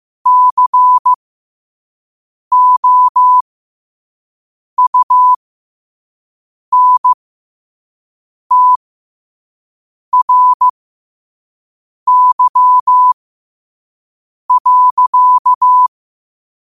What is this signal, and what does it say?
Morse code, heard as a keyed tone, about 15 wpm, spaced as for 5 wpm, 1000 Hz. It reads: COUNTRY.
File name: qotdfiles / 5wpm